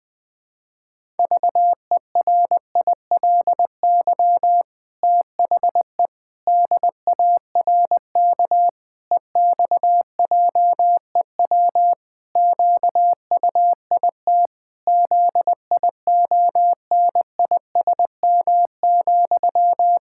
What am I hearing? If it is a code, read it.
Morse code, heard as a keyed tone, about 19 wpm, 700 Hz. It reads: VERILY THE DARK EXJEW QUIT ZIONISM,
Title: Pangram Challenge v2 - 20wpm